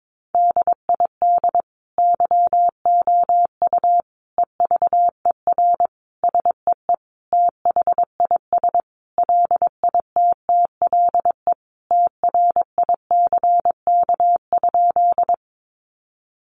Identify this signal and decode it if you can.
Morse code, heard as a keyed tone, about 22 wpm, 700 Hz. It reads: DID YOU EVER SEE THIS LITTLE TRICK?
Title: Sentences from Top 1500 Words 22wpm